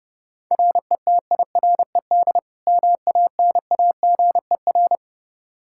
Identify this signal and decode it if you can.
Morse code, heard as a keyed tone, about 30 wpm, 700 Hz. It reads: RETIRED MANAGER
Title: K1USN SST QSO Practice 30wpm